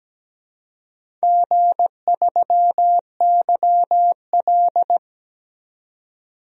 Morse code, heard as a keyed tone, about 17 wpm, 700 Hz. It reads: G3YL
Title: DXCC-100 Call Signs - 4 Characters - 17wpm